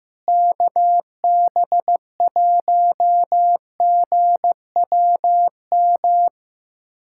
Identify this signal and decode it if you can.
Morse code, heard as a keyed tone, about 15 wpm, 700 Hz. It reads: KB1GWM
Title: USA Callsign DE Callsign - 6 Characters - 15wpm